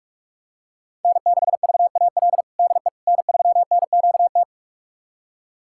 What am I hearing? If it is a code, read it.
Morse code, heard as a keyed tone, about 47 wpm, 700 Hz. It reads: N6VAL DE N3NQT
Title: USA Callsign DE Callsign - 5 Characters - 45wpm